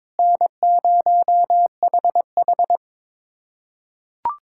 Morse code, heard as a keyed tone, about 22 wpm, 700 Hz. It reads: N0HH
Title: USA Callsign DE Callsign - 4 Characters - 22wpm